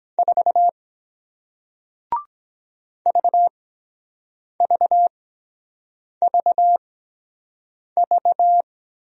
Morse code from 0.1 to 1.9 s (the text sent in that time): 4